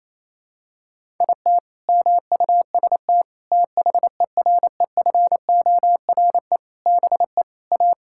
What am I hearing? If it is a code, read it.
Morse code, heard as a keyed tone, about 28 wpm, 700 Hz. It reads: IT MUST THEREFORE BE A
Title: Sentences from Top 1000 Words - Review Part 6 - 28wpm